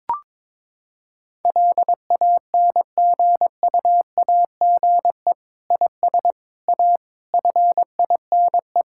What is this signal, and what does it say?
Morse code, heard as a keyed tone, about 22 wpm, 700 Hz. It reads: LANGUAGE IS A FINE
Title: Sentences from Top 1000 Words - Review Part 6 - 22wpm